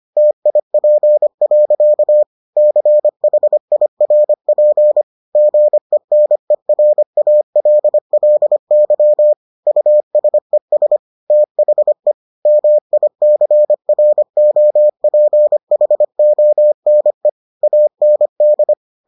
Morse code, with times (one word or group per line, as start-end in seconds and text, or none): 0.2-2.2 TIP.
2.6-5.0 CHIRP
5.4-9.3 GENERALLY
9.7-11.0 USES
11.3-12.1 THE
12.5-17.3 MICROPHONE
17.6-18.7 AND